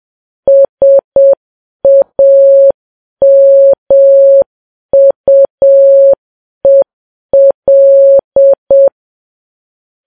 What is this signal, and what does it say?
Morse code, heard as a keyed tone, about 7 wpm, 550 Hz. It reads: SAMUEL